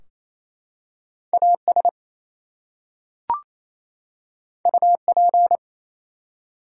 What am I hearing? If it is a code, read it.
Morse code, heard as a keyed tone, about 27 wpm, 700 Hz. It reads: AS UP
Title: Two Letter Words - 28wpm